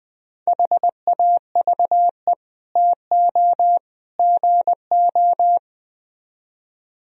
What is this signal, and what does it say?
Morse code, heard as a keyed tone, about 20 wpm, 700 Hz. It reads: HAVE TO GO
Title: Speed-Racing - Sets of 3 Words 17wpm